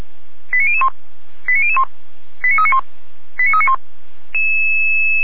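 Five-tone selective call (ZVEI), heard as quick runs of five tones, tones of about 70 ms. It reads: 89001 89001 89391 89391
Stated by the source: ZVEI five-tone call received on SDR.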